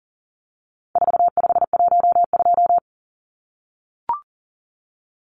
Morse code, heard as a keyed tone, about 40 wpm, 700 Hz. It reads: 4512
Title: Numbers - 4-Digit 40wpm